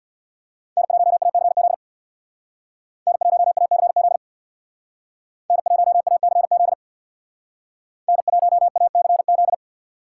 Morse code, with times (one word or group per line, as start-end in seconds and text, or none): 0.8-1.7 N1ACZ
3.1-4.2 N1ACZ
5.5-6.7 N1ACZ
8.1-9.5 N1ACZ